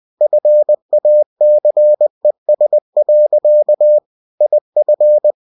Morse code, heard as a keyed tone, about 20 wpm, 600 Hz. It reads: FACES. IF
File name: cw-books / winnie-the-pooh / winnie-20wpm-page-0072